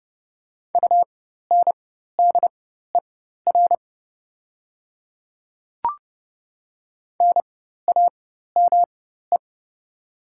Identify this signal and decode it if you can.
Morse code, heard as a keyed tone, about 30 wpm, spaced as for 14 wpm, 700 Hz. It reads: UNDER NAME
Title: 4x Character Spacing - Top 200 Words - 30wpm